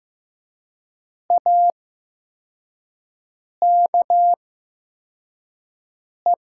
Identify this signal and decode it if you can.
Morse code, heard as a keyed tone, about 15 wpm, spaced as for 4 wpm, 700 Hz. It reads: AKE
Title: 8x Character Spacing - Top 100 Words - 15wpm